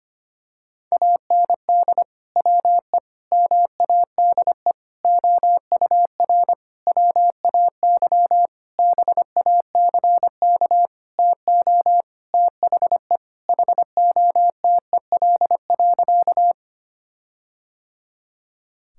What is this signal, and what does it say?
Morse code, heard as a keyed tone, about 25 wpm, 700 Hz. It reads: AND WE MADE OUR WAY BACK TO THE HOTEL.
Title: Sentences from Top 1600 Words 25wpm